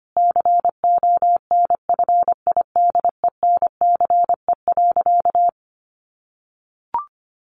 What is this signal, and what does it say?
Morse code, heard as a keyed tone, about 25 wpm, 700 Hz. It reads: CONFIDENCE.
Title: Sentences from Top 1500 Words 25wpm